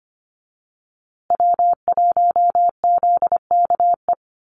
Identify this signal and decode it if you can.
Morse code, heard as a keyed tone, about 25 wpm, 700 Hz. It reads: W1ZKE
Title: USA Call Signs Repeated 3x 25wpm